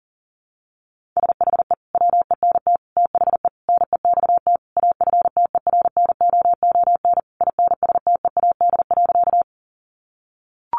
Morse code, heard as a keyed tone, about 37 wpm, 700 Hz. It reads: SHE WENT THE NEXT AFTERNOON INSTEAD.
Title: Sentences from Top 1500 Words 40wpm